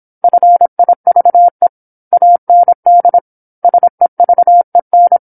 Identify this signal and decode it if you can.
Morse code, heard as a keyed tone, about 26 wpm, 700 Hz. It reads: FIVE AND SEVEN
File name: 08-Chapter-08 26 WPM